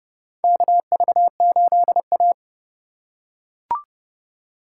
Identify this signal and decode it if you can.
Morse code, heard as a keyed tone, about 31 wpm, 700 Hz. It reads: KV8A
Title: USA Call Signs - 4 Characters - 4x Character Spacing - 30wpm